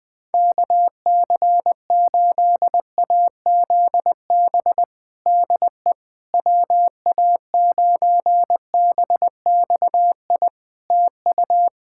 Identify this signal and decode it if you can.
Morse code, heard as a keyed tone, about 20 wpm, 700 Hz. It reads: KC8AZB DE WA9BXI TU
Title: K1USN SST QSO Practice 20wpm